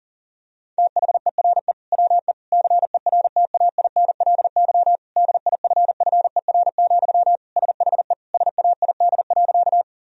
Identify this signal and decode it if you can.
Morse code, heard as a keyed tone, about 40 wpm, 700 Hz. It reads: THERE WE CERTAINLY DIFFER, SHE SAID.
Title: Sentences from Top 1000 Words - Review Part 1 - 40wpm